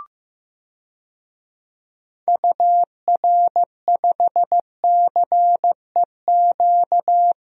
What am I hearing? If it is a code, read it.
Morse code, heard as a keyed tone, about 15 wpm, 700 Hz. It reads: UR5CEQ
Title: DXCC-100 Call Signs - 6 Characters - Repeated 2x - 15wpm